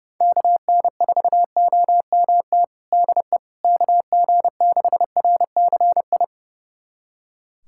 Morse code, heard as a keyed tone, about 32 wpm, 700 Hz. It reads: KN4OMT DE KG6RCI
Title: USA Callsign DE Callsign - 6 Characters - 30wpm